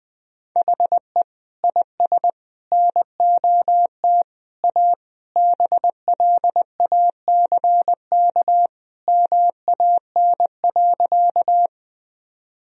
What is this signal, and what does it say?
Morse code, heard as a keyed tone, about 21 wpm, 700 Hz. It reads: HE IS NOT A BLACK MAN.
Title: Sentences from Top 1000 Words - Review Part 2 - 20wpm